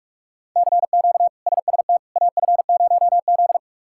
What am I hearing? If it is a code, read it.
Morse code, heard as a keyed tone, about 41 wpm, 700 Hz. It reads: CQ SST AF0Z
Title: K1USN SST QSO Practice - 45wpm